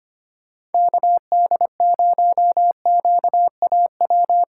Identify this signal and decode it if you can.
Morse code, heard as a keyed tone, about 31 wpm, 700 Hz. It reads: KD0QAW